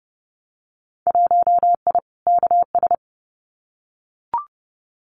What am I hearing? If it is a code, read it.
Morse code, heard as a keyed tone, about 30 wpm, 700 Hz. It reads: 1I KS
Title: Winter Field Day Exchanges Encore 30wpm